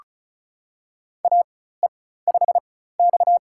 Morse code, heard as a keyed tone, about 36 wpm, spaced as for 16 wpm, 700 Hz. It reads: AE5X
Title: USA Call Signs - 4 Characters - 4x Character Spacing - 35wpm